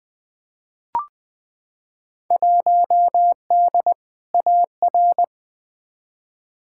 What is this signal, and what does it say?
Morse code, heard as a keyed tone, about 20 wpm, 700 Hz. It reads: 1D AR
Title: Field Day Exchanges Encore - 20wpm